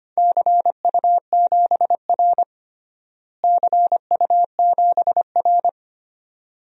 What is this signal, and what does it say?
Morse code, heard as a keyed tone, about 25 wpm, 700 Hz. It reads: CU7R CU7R
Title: DXCC-100 Call Signs - 4 Characters - Repeated 2x - 25wpm